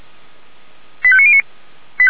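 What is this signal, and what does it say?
Five-tone selective call (ZVEI), heard as quick runs of five tones, tones of about 70 ms. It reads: 85909
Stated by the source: ZVEI five-tone call received by SDR, from the audio server of Feuerwehr Orlamünde.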